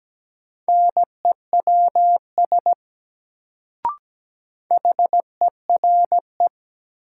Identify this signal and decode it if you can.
Morse code, heard as a keyed tone, about 17 wpm, 700 Hz. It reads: NEWS HERE
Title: Sets of 5 Words - New Word Drill - 17wpm